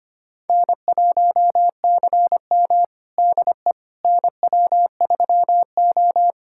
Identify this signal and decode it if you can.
Morse code, heard as a keyed tone, about 25 wpm, 700 Hz. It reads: N1CM DE NW3O
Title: USA Callsign DE Callsign - 4 Characters - 25wpm